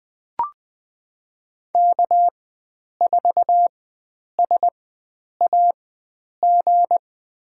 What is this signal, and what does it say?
Morse code, heard as a keyed tone, about 20 wpm, spaced as for 9 wpm, 700 Hz. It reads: K4SAG
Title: USA Call Signs - 4x Character Spacing - 20wpm